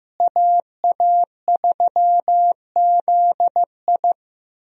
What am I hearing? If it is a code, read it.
Morse code, heard as a keyed tone, about 15 wpm, 700 Hz. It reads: AA3ZI
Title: USA Call Signs - 5 Characters - Repeated 3x - 15wpm